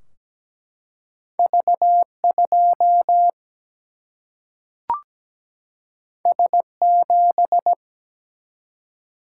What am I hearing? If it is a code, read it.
Morse code, heard as a keyed tone, about 17 wpm, 700 Hz. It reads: V 2 S 7
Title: Single-Letter Single-Number 17wpm